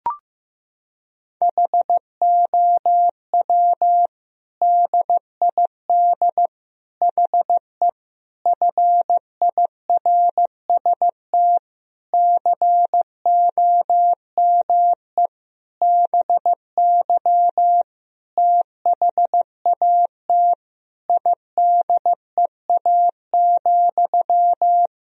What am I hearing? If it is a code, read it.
Morse code, heard as a keyed tone, about 14 wpm, 700 Hz. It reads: HOW DID HE FIRST COME BY THAT IDEA,
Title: Sentences from Top 1000 Words - Review Part 2 - 15wpm